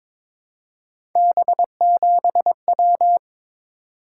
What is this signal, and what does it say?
Morse code, heard as a keyed tone, about 22 wpm, 700 Hz. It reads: B7W
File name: practice / 22wpm / Three Letter-Number Combinations - 22wpm